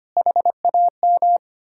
Morse code, heard as a keyed tone, about 33 wpm, 700 Hz. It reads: HAM